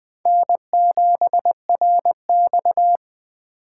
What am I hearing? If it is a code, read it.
Morse code, heard as a keyed tone, about 20 wpm, 700 Hz. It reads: N7RX
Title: Four Letter-Number Combinations - 20wpm